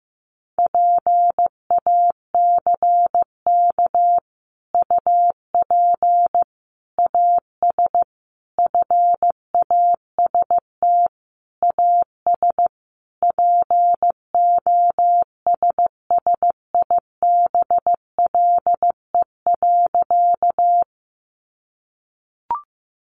Morse code, heard as a keyed tone, about 15 wpm, 700 Hz. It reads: PACK UP AS FAST AS POSSIBLE.
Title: Sentences from Top 1300 Words 15wpm